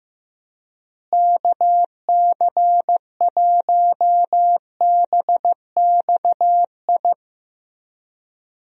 Morse code, heard as a keyed tone, about 15 wpm, 700 Hz. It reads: KC1BXI
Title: USA Call Signs - 6 Characters - Repeated 2x - 15wpm